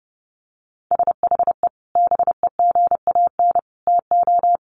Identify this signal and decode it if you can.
Morse code, heard as a keyed tone, about 30 wpm, 700 Hz. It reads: SHE BEGAN TO
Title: Sentences from Top 500 Words 30wpm